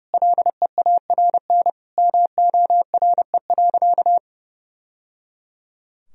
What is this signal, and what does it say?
Morse code, heard as a keyed tone, about 30 wpm, 700 Hz. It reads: LEARN MORE.
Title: Sentences from Top 1000 Words - Review Part 4 - 30wpm